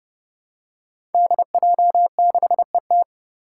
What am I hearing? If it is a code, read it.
Morse code, heard as a keyed tone, about 30 wpm, 700 Hz. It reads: DJ6ET
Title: Five Challenging Letter-Number Combinations 30wpm